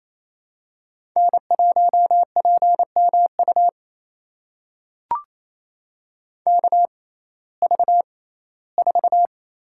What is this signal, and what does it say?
Morse code, heard as a keyed tone, about 28 wpm, spaced as for 17 wpm, 700 Hz. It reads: N1PMU KV4
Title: USA Call Signs - 5 Characters - 6x Character Spacing - 28wpm